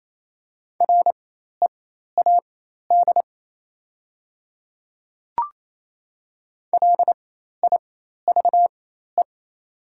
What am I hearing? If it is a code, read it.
Morse code, heard as a keyed tone, about 27 wpm, spaced as for 13 wpm, 700 Hz. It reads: READ LIVE
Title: 4x Character Spacing - Top 300 Words - 28wpm